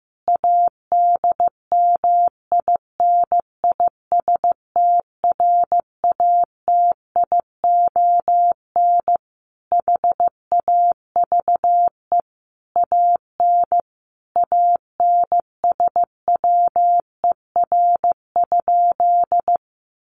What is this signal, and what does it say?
Morse code, heard as a keyed tone, about 14 wpm, 700 Hz. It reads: ADMINISTRATION HAVE AN ANSWER?
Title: Sentences from Top 1300 Words 15wpm